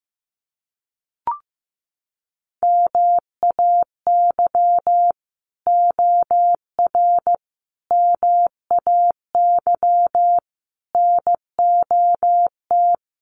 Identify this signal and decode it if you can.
Morse code, heard as a keyed tone, about 15 wpm, 700 Hz. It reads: MAY OR MAY NOT
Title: Sets of 5 Words Encore2 15wpm